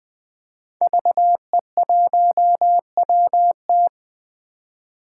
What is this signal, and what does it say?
Morse code, heard as a keyed tone, about 20 wpm, 700 Hz. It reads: VE1WT
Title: 2023 SOTA Chaser Call Signs - 20wpm